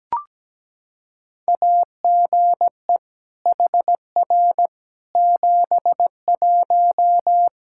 Morse code, heard as a keyed tone, about 17 wpm, 700 Hz. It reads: AGE HR 71
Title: Practice QSO - Exchange 3 - More Info - 17wpm